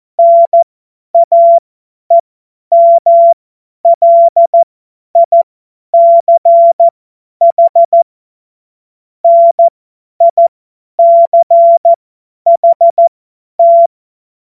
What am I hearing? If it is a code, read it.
Morse code, heard as a keyed tone, about 14 wpm, spaced as for 10 wpm, 680 Hz. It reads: NAEMLICH NICHT
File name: Morsekurs / Witze / Witz 28. September 2021, 50 F70